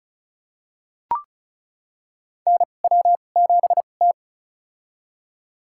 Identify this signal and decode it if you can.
Morse code, heard as a keyed tone, about 34 wpm, spaced as for 25 wpm, 700 Hz. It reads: NW7T
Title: USA Call Signs - 2x Character Spacing - 35wpm